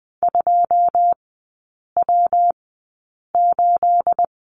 Morse code, heard as a keyed tone, about 20 wpm, spaced as for 8 wpm, 700 Hz. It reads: 2W8
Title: 212 Lesson 34e - Number 8 - Three Characters 20wpm